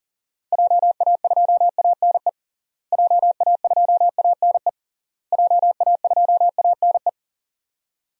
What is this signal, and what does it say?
Morse code, heard as a keyed tone, about 40 wpm, 700 Hz. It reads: JA2ANE JA2ANE JA2ANE
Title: DXCC-100 Call Signs - 6 Characters - Repeated 3x - 40wpm